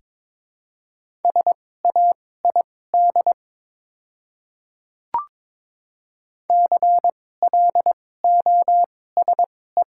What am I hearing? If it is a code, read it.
Morse code, heard as a keyed tone, about 22 wpm, spaced as for 16 wpm, 700 Hz. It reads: SAID CLOSE
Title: 2x Character Spacing - Top 300 Words - 22wpm